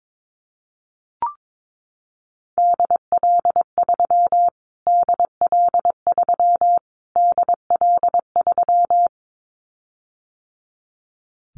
Morse code, heard as a keyed tone, about 22 wpm, 700 Hz. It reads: DL3 DL3 DL3
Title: Callsign Prefixes for Top 25 DXCC - Repeated 3x - 22wpm